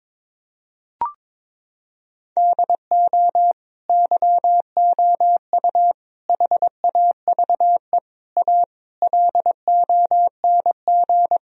Morse code, heard as a keyed tone, about 21 wpm, 700 Hz. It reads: DO YOU HAVE A LONG
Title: Sentences from Top 1000 Words - Review Part 4 - 22wpm